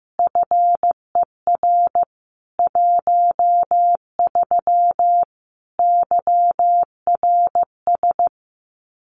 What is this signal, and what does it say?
Morse code, heard as a keyed tone, about 15 wpm, 700 Hz. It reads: FER 13 YRS
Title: Common QSO Phrases 15wpm